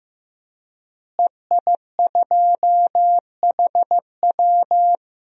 Morse code, heard as a keyed tone, about 15 wpm, 700 Hz. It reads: EI2HW
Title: DXCC-100 Call Signs - 5 Characters - Repeated 3x - 15wpm